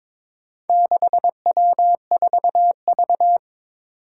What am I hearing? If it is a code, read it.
Morse code, heard as a keyed tone, about 22 wpm, 700 Hz. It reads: 6W4V
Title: Four Challenging Letter-Number Combinations 22wpm